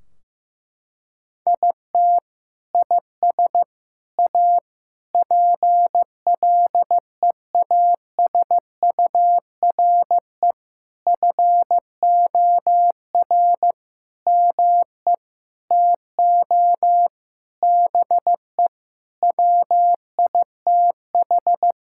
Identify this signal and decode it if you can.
Morse code, heard as a keyed tone, about 15 wpm, 700 Hz. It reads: IT IS A PLEASURE FOR ME TO BE WITH
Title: Sentences from Top 1600 Words 15wpm